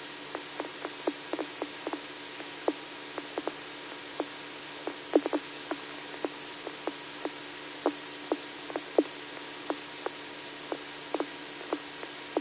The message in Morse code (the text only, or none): none